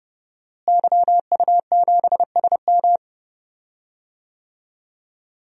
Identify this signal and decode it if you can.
Morse code, heard as a keyed tone, about 32 wpm, 700 Hz. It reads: YU7SM
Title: DXCC-100 Call Signs - 5 Characters - Repeated 3x - 30wpm